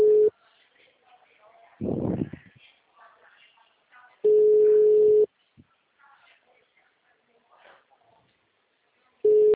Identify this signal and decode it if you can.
Ringback tone (Europe, 425 Hz), with speech.
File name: out- -2013- - - .39890